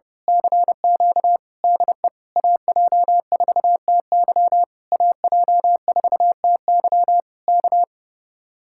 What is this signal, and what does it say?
Morse code, heard as a keyed tone, about 28 wpm, 700 Hz. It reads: CQ DE AJ4TY AJ4TY K
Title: Speed-Racing Common QSO Phrases - 30wpm